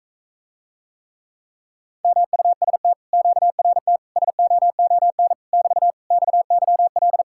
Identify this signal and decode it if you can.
Morse code, heard as a keyed tone, about 45 wpm, 700 Hz. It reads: MUST QRT SOON <BT> XYL